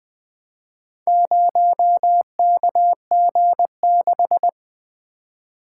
Morse code, heard as a keyed tone, about 20 wpm, 700 Hz. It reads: 0KG6
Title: Four Letter-Number Combinations - 20wpm